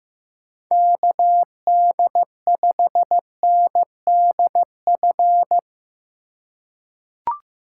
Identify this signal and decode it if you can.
Morse code, heard as a keyed tone, about 15 wpm, 700 Hz. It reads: KD5NDF
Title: USA Call Signs - 2x Character Spacing - 15wpm